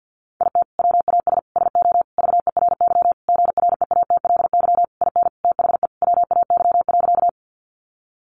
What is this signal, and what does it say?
Morse code, heard as a keyed tone, about 50 wpm, 700 Hz. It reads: IT WAS SO VERY GREATLY IN THE WAY.
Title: Sentences from Top 1400 Words 50wpm